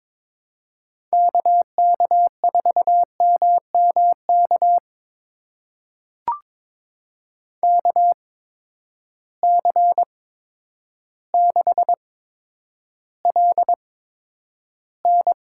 Morse code, heard as a keyed tone, about 22 wpm, spaced as for 10 wpm, 700 Hz. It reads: KK4MMK KC6LN
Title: USA Call Signs - 6 Characters - 8x Character Spacing - 22wpm